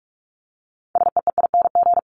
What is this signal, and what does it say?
Morse code, heard as a keyed tone, about 44 wpm, 700 Hz. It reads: SEEING
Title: ING Suffix - Top 100 Words with Base Words 45wpm